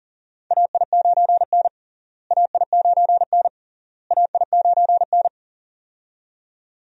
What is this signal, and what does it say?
Morse code, heard as a keyed tone, about 40 wpm, 700 Hz. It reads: AI9N AI9N AI9N
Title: USA Call Signs - 4 Characters - Repeated 3x - 40wpm